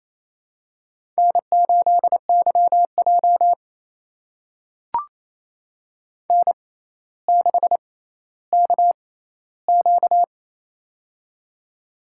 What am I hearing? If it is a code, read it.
Morse code, heard as a keyed tone, about 29 wpm, spaced as for 14 wpm, 700 Hz. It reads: N8YJ N6KQ
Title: USA Call Signs - 4 Characters - 6x Character Spacing - 28wpm